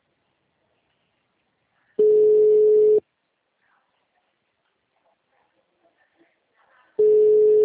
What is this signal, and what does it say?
Ringback tone (Europe, 425 Hz).
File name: out- -2019- - - .49416